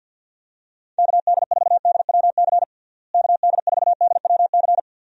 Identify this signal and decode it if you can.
Morse code, heard as a keyed tone, about 48 wpm, 700 Hz. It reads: KD4DWC KD4DWC
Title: USA Call Signs - 6 Characters - Repeated 3x - 50wpm